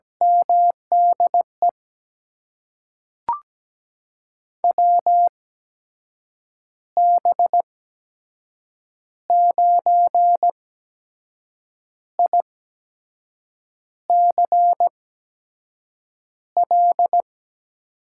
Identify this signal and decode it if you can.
Morse code, heard as a keyed tone, about 17 wpm, spaced as for 6 wpm, 700 Hz. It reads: MDE WB9ICL
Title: USA Call Signs - 6 Characters - 8x Character Spacing - 17wpm